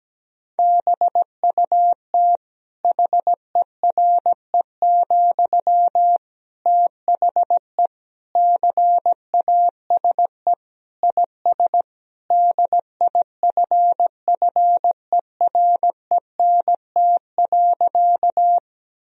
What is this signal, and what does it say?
Morse code, heard as a keyed tone, about 17 wpm, 700 Hz. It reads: BUT HERE, THE CASE IS DIFFERENT.
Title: Sentences from Top 1000 Words - Review Part 3 - 17wpm